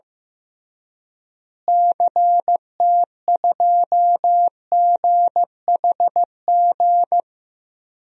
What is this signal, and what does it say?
Morse code, heard as a keyed tone, about 16 wpm, 700 Hz. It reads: CT2GHG
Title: DXCC-100 Call Signs - 6 Characters - Repeated 3x - 15wpm